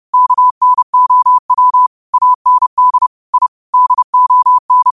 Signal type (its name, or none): Morse code